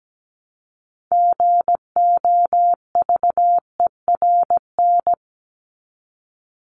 Morse code, heard as a keyed tone, about 17 wpm, 700 Hz. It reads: GOVERN
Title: Top 500 Words 17wpm